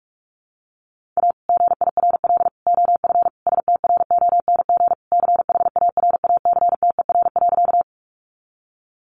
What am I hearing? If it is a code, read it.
Morse code, heard as a keyed tone, about 44 wpm, 700 Hz. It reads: A GIRL OF STRONG CHARACTER.
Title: Sentences from Top 1200 Words 45wpm